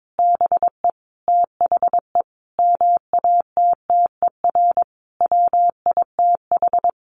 Morse code, heard as a keyed tone, about 22 wpm, 700 Hz. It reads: BE THE MATTER WITH
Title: Sentences from Top 700 Words 22wpm